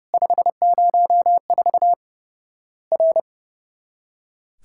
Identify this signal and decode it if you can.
Morse code, heard as a keyed tone, about 32 wpm, 700 Hz. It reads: 504 R
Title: Worked All Europe DX Contest - QTC Exchange - 30wpm